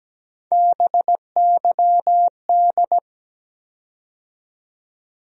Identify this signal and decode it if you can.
Morse code, heard as a keyed tone, about 17 wpm, 700 Hz. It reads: BYD
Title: USA Call Sign Postfixes - 3 Letters 17wpm